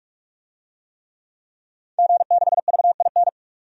Morse code, heard as a keyed tone, about 43 wpm, 700 Hz. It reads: G6VIN
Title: DXCC-100 Call Signs - 5 Characters - 45wpm